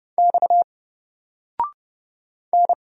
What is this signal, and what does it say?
Morse code, heard as a keyed tone, about 30 wpm, 700 Hz. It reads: X N
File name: Single-Letter-Number-30wpm